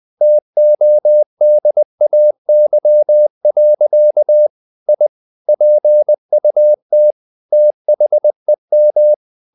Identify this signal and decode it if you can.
Morse code, heard as a keyed tone, about 20 wpm, 600 Hz. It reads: TODAY. I PUT THEM
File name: the old man and the sea / the old man and the sea part 1 20wpm